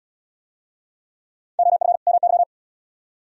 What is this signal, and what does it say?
Morse code, heard as a keyed tone, about 69 wpm, 700 Hz. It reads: OF MY